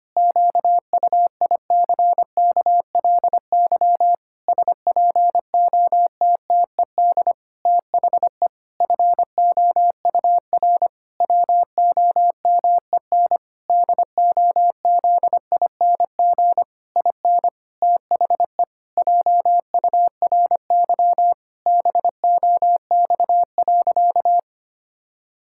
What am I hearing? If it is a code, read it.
Morse code, heard as a keyed tone, about 25 wpm, 700 Hz. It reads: QUICKLY SPOTTED THE FOUR WOMEN DOZING IN THE JURY BOX.
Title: Pangram Challenge v2 - 25wpm